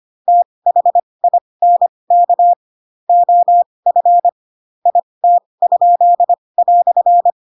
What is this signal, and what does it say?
Morse code, heard as a keyed tone, about 25 wpm, spaced as for 20 wpm, 700 Hz. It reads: THINK OF IT?"
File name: Pinocchio / Pinocchio ch2 15 25 ik1hge